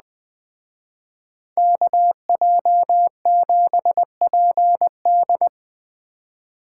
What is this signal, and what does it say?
Morse code, heard as a keyed tone, about 20 wpm, 700 Hz. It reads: KJ7PD